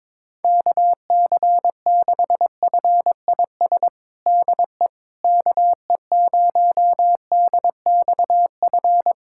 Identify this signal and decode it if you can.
Morse code, heard as a keyed tone, about 21 wpm, 700 Hz. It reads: KC6FIS DE KE0DXF
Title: USA Callsign DE Callsign - 6 Characters - 22wpm